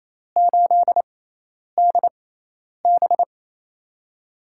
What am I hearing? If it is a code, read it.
Morse code, heard as a keyed tone, about 27 wpm, spaced as for 10 wpm, 700 Hz. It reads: 8DB
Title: USA Call Signs - 4 Characters - 6x Character Spacing - 28wpm